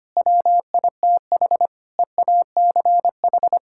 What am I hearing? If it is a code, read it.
Morse code, heard as a keyed tone, about 25 wpm, 700 Hz. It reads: WITH EACH
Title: Sentences from Top 1000 Words - Review Part 6 - 25wpm